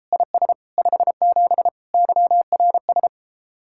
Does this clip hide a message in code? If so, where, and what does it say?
Yes: Morse code from 0.1 to 3.1 s, heard as a keyed tone, about 34 wpm, 700 Hz. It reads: IS 57 YRS